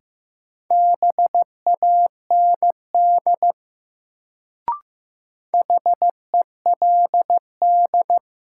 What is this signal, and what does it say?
Morse code, heard as a keyed tone, about 16 wpm, 700 Hz. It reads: BAND HELD
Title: Sentences from Top 1600 Words - New Word Drill Alt - 15wpm